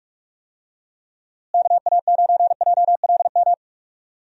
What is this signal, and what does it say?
Morse code, heard as a keyed tone, about 47 wpm, 700 Hz. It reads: KA9JLM
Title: USA Call Signs - 6 Characters - Repeated 2x - 45wpm